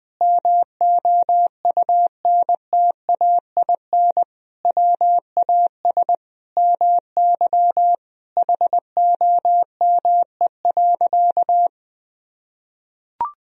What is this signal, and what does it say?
Morse code, heard as a keyed tone, about 20 wpm, 700 Hz. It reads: MOUNTAIN WAS MY HOME.
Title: Sentences from Top 1000 Words - Review Part 1 - 20wpm